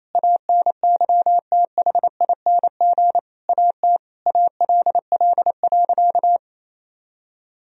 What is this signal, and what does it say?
Morse code, heard as a keyed tone, about 28 wpm, 700 Hz. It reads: ANYTHING AT ALL.
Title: Sentences from Top 1000 Words - Review Part 1 - 28wpm